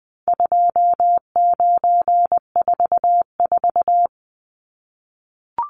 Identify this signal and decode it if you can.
Morse code, heard as a keyed tone, about 20 wpm, 700 Hz. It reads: 2944
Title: Numbers - 4-Digit 20wpm